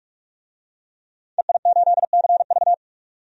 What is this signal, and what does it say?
Morse code, heard as a keyed tone, about 42 wpm, 700 Hz. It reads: EI8CV
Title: DXCC-100 Call Signs - 5 Characters - 45wpm